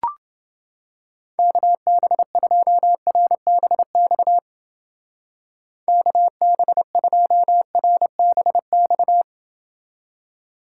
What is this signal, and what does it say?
Morse code, heard as a keyed tone, about 29 wpm, 700 Hz. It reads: KB2RBX KB2RBX